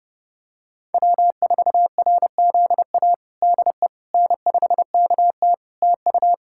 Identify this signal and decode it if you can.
Morse code, heard as a keyed tone, about 31 wpm, 700 Hz. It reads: W4RZA DE N5KT TU
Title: Practice QSO - Exchange 4 - Ending - 30wpm